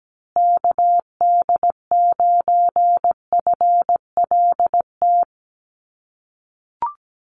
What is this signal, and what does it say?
Morse code, heard as a keyed tone, about 17 wpm, 700 Hz. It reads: KD9FLT
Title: USA Call Signs Repeated 3x 17wpm